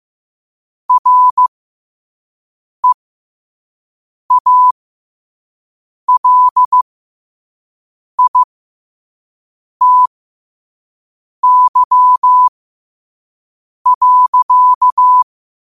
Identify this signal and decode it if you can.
Morse code, heard as a keyed tone, about 15 wpm, spaced as for 5 wpm, 1000 Hz. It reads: REALITY.
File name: qotdfiles / 5wpm